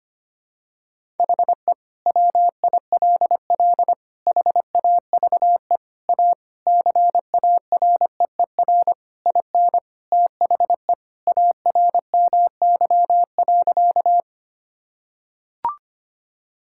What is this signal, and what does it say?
Morse code, heard as a keyed tone, about 25 wpm, 700 Hz. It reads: HE WILL HAVE A CAREER IN THE ARMY.
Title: Sentences from Top 1000 Words - Review Part 6 - 25wpm